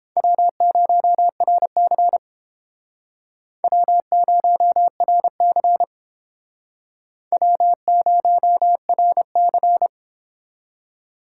Morse code, heard as a keyed tone, about 30 wpm, 700 Hz. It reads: W0RC W0RC W0RC